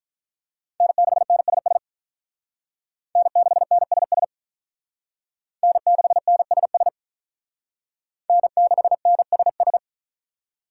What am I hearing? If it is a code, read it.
Morse code, heard as a keyed tone, about 46 wpm, 700 Hz. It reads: N6NSS N6NSS N6NSS N6NSS